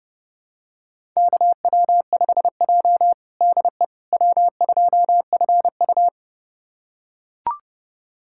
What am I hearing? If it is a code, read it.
Morse code, heard as a keyed tone, about 30 wpm, 700 Hz. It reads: KW5J DE W2FU
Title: USA Callsign DE Callsign - 4 Characters - 30wpm